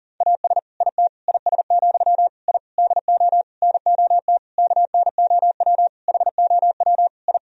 Morse code, heard as a keyed tone, about 40 wpm, 700 Hz. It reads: AS IT IS, I DO NOT KNOW HOW I
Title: Sentences from Top 1000 Words - Review Part 2 - 40wpm